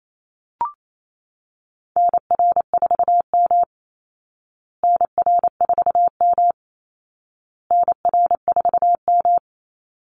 Morse code, heard as a keyed tone, about 28 wpm, 700 Hz. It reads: NR4M NR4M NR4M
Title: Call Signs Repeated 2x - 28wpm